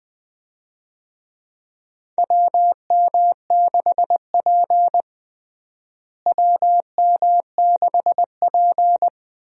Morse code, heard as a keyed tone, about 20 wpm, 700 Hz. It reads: WM6P WM6P
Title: USA Call Signs - 4 Characters - Repeated 3x - 20wpm